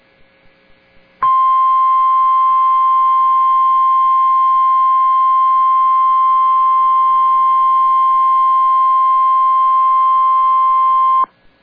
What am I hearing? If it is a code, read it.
Emergency Alert System (SAME) tones (the 1050 Hz attention tone). It is heard from 1.2 to 11.3 s.